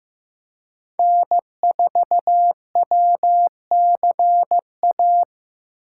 Morse code, heard as a keyed tone, about 15 wpm, 700 Hz. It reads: N4WCA
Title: USA Call Signs - 3x Character Spacing - 15wpm